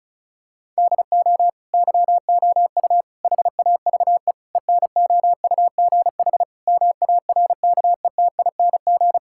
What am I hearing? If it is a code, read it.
Morse code, heard as a keyed tone, about 34 wpm, 700 Hz. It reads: DO YOU HAVE ENOUGH MARKETING
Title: Sentences from Top 1000 Words - Review Part 5 - 35wpm